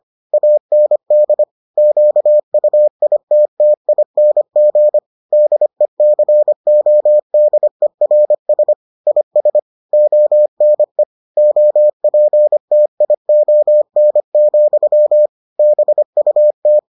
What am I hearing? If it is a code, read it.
Morse code, heard as a keyed tone, about 25 wpm, 600 Hz. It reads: AND QUITTING DECODERS IS ONE OPTION, BUT